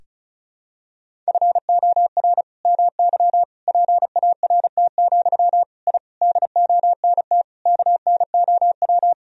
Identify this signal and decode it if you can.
Morse code, heard as a keyed tone, about 36 wpm, 700 Hz. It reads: FOR MY PART, I DONT KNOW
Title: Sentences from Top 1000 Words - Review Part 1 - 35wpm